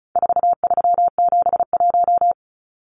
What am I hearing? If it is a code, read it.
Morse code, heard as a keyed tone, about 34 wpm, 700 Hz. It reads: 4371
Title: Numbers - 4-Digit 35wpm